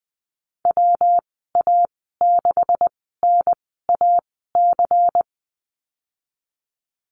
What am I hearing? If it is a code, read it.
Morse code, heard as a keyed tone, about 20 wpm, spaced as for 15 wpm, 700 Hz. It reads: WA6NAC
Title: 179 Lesson 29g - Slash - Call Signs 20wpm